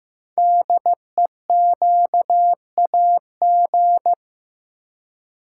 Morse code, heard as a keyed tone, about 15 wpm, 690 Hz. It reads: DEQAG
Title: Code Groups - 5 Letters - 15wpm